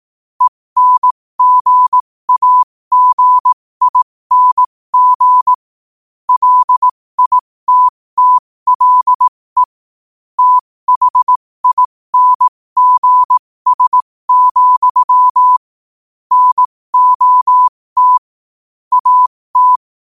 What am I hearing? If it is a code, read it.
Morse code, heard as a keyed tone, about 18 wpm, spaced as for 15 wpm, 1000 Hz. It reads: ENGAGING LITTLE THINGS, NOT AT
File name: qotdfiles / 15wpm